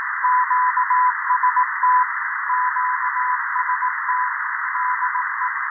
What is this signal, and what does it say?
Morse code, heard as a keyed tone, about 18 wpm, 1000 Hz, with noise. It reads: QST QST DE
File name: 0340z-fmt k5cm 80m by pa0pje